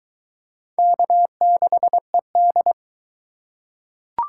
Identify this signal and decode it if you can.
Morse code, heard as a keyed tone, about 20 wpm, 700 Hz. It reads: K6ED